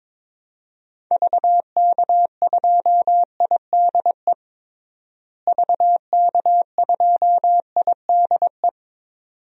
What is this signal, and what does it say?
Morse code, heard as a keyed tone, about 22 wpm, 700 Hz. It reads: VK2IDE VK2IDE
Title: DXCC-100 Call Signs - 6 Characters - Repeated 2x - 22wpm